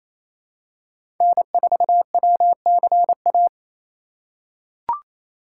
Morse code, heard as a keyed tone, about 27 wpm, 700 Hz. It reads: N4WCA
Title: USA Call Signs - 5 Characters - 4x Character Spacing - 28wpm